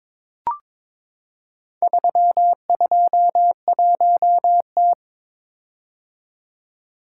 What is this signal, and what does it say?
Morse code, heard as a keyed tone, about 22 wpm, 690 Hz. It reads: 321T
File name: practice / 22wpm / Four Letter-Number Combinations - 22wpm